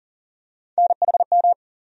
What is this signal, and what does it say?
Morse code, heard as a keyed tone, about 40 wpm, 700 Hz. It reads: NHM
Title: USA Call Sign Postfixes - 3 Letters 40wpm